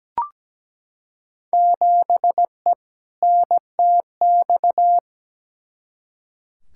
Morse code, heard as a keyed tone, about 17 wpm, 700 Hz. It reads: 7E NTX
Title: Field Day Exchanges Encore - 17wpm